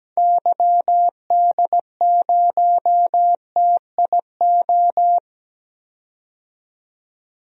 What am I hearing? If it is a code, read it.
Morse code, heard as a keyed tone, about 17 wpm, 700 Hz. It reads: YD0TIO
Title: DXCC-100 Call Signs - 6 Characters - 17wpm